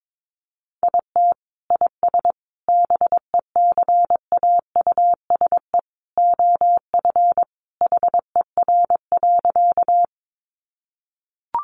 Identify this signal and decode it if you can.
Morse code, heard as a keyed tone, about 22 wpm, 700 Hz. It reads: IT IS BECAUSE OF HER.
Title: Sentences from Top 600 Words 22wpm